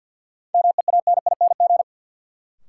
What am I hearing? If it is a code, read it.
Morse code, heard as a keyed tone, about 47 wpm, 700 Hz. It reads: MEANING
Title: Sentences from Top 900 Words - New Word Drill - 50wpm